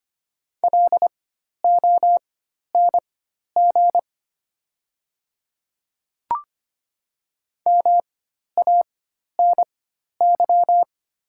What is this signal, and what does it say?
Morse code, heard as a keyed tone, about 25 wpm, spaced as for 12 wpm, 700 Hz. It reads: LONG MANY
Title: 4x Character Spacing - Top 100 Words - 25wpm